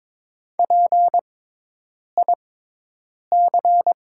Morse code, heard as a keyed tone, about 22 wpm, spaced as for 8 wpm, 700 Hz. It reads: PIC
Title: 6x Character Spacing - Top 300 Words - 22wpm